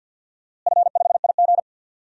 Morse code, heard as a keyed tone, about 52 wpm, 700 Hz. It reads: F5IG
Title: DXCC-100 Call Signs - 4 Characters - Repeated 3x - 50wpm